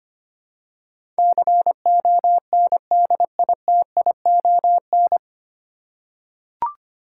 Morse code, heard as a keyed tone, about 25 wpm, 700 Hz. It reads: CONDITION
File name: Nine Letter Words - 25wpm